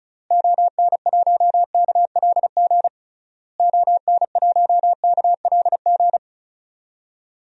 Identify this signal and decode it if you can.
Morse code, heard as a keyed tone, about 36 wpm, 700 Hz. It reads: ON1KLG ON1KLG
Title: DXCC-100 Call Signs - 6 Characters - Repeated 3x - 35wpm